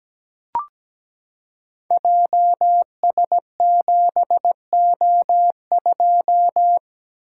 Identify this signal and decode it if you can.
Morse code, heard as a keyed tone, about 17 wpm, 700 Hz. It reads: JS7O2
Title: Code Groups - 5 Letter:Numbers - 17wpm